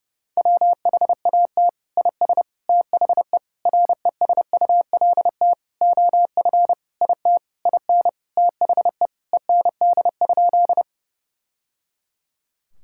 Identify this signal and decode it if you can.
Morse code, heard as a keyed tone, about 30 wpm, 700 Hz. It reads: WHAT IS THE RESULT OF IT IN THE END?
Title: Sentences from Top 1000 Words - Review Part 5 - 30wpm